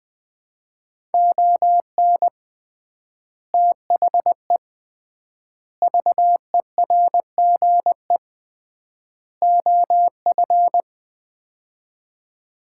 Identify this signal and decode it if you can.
Morse code, heard as a keyed tone, about 20 wpm, 700 Hz. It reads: ON THE VERGE OF
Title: 3x Word-Spacing - Sets of 4 Words - 20wpm